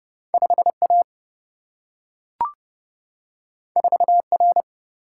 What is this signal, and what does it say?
Morse code, heard as a keyed tone, about 29 wpm, 700 Hz. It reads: 5A 4R
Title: Single-Letter Single-Number 30wpm